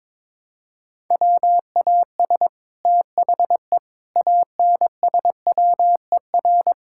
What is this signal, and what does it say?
Morse code, heard as a keyed tone, about 21 wpm, 700 Hz. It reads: WAS THE ANSWER
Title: Sentences from Top 1000 Words - Review Part 6 - 22wpm